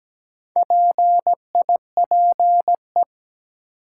Morse code, heard as a keyed tone, about 17 wpm, 700 Hz. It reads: PIPE
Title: Sentences from Top 1500 Words - New Word Drill - 17wpm